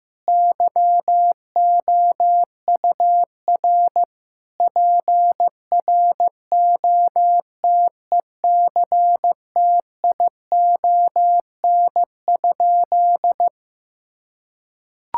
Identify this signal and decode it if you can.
Morse code, heard as a keyed tone, about 16 wpm, 700 Hz. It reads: YOUR PROTECTION?
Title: Sentences from Top 1600 Words 15wpm